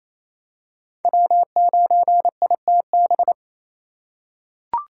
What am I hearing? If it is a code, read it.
Morse code, heard as a keyed tone, about 28 wpm, 700 Hz. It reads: W9ITB
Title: USA Call Signs - 5 Characters - 4x Character Spacing - 28wpm